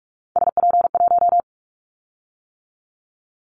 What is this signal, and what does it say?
Morse code, heard as a keyed tone, about 46 wpm, 700 Hz. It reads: SP1
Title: Callsign Prefixes for Top 25 DXCC - 45wpm